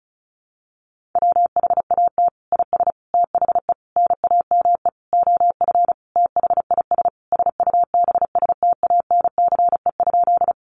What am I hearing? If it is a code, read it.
Morse code, heard as a keyed tone, about 34 wpm, 700 Hz. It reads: WHAT IS THE NAME OF THIS SUBSTANCE?
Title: Sentences from Top 1500 Words 35wpm